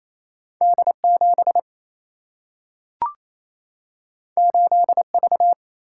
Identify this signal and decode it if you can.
Morse code, heard as a keyed tone, about 28 wpm, 700 Hz. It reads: D7 8V
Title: Single-Letter Single-Number 28wpm